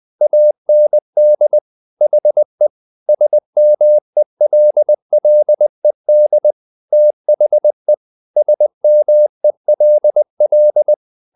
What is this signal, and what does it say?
Morse code, heard as a keyed tone, about 20 wpm, 600 Hz. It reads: AND HE SMELLED THE SMELL
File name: the old man and the sea / the old man and the sea part 1 20wpm